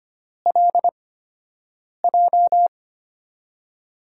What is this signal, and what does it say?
Morse code, heard as a keyed tone, about 25 wpm, spaced as for 7 wpm, 700 Hz. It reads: LJ
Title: USA Call Signs - 6 Characters - 8x Character Spacing - 25wpm